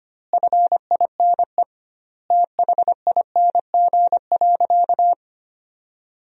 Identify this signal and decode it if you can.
Morse code, heard as a keyed tone, about 25 wpm, 700 Hz. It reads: FINE THING.
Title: 2x Word-Spacing - Sentences from Top 500 Words - 25wpm